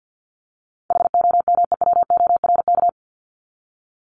Morse code, heard as a keyed tone, about 50 wpm, 700 Hz. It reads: HOMEWORK
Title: Top 1501-1750 Words 50wpm